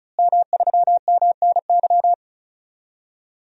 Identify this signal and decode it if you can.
Morse code, heard as a keyed tone, about 36 wpm, 700 Hz. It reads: M3MNY
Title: DXCC-100 Call Signs - 5 Characters - Repeated 3x - 35wpm